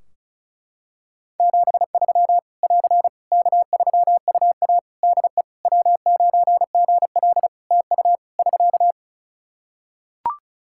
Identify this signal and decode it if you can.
Morse code, heard as a keyed tone, about 36 wpm, 700 Hz. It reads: 73 <AR> K3UA DE W9GL TU <SK>
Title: Practice QSO - Exchange 4 - Ending - 35wpm